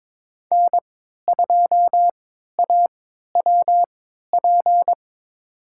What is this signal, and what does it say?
Morse code, heard as a keyed tone, about 23 wpm, spaced as for 13 wpm, 700 Hz. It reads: N2AWP
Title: USA Call Signs - 5 Characters - 3x Character Spacing - 22wpm